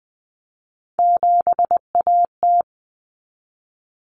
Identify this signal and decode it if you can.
Morse code, heard as a keyed tone, about 20 wpm, 700 Hz. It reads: 7AT
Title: Common Cut Numbers - 3-Digit 20wpm